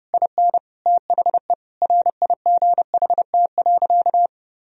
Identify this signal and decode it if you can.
Morse code, heard as a keyed tone, about 32 wpm, 700 Hz. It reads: IN THE RIGHT.
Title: Sentences from Top 1000 Words - Review Part 1 - 30wpm